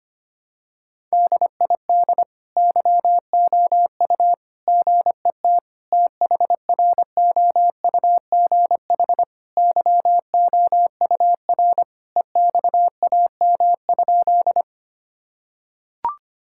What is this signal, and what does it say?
Morse code, heard as a keyed tone, about 25 wpm, 700 Hz. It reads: DID YOU GET THROUGH YOUR EXAM?
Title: Sentences from Top 1000 Words - Review Part 5 - 25wpm